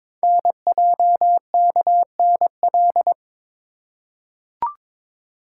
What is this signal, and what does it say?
Morse code, heard as a keyed tone, about 22 wpm, 700 Hz. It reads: NJKNL
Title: Code Groups - 5 Letter:Numbers - 22wpm